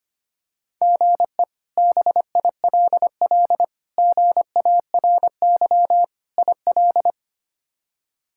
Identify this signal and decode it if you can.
Morse code, heard as a keyed tone, about 25 wpm, 700 Hz. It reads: GE BILL GARY IL
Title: K1USN SST QSO Practice - 25wpm